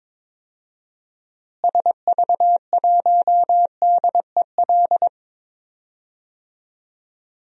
Morse code, heard as a keyed tone, about 22 wpm, 700 Hz. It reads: SV1DEL
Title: DXCC-100 Call Signs - 6 Characters - Repeated 2x - 22wpm